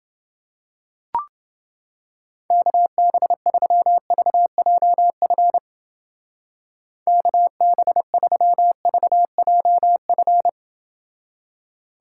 Morse code, heard as a keyed tone, about 30 wpm, 700 Hz. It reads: KB3VJF KB3VJF